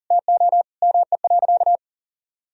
Morse code, heard as a keyed tone, about 40 wpm, 700 Hz. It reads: TO ME.
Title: Sentences from Top 1000 Words - Review Part 4 - 40wpm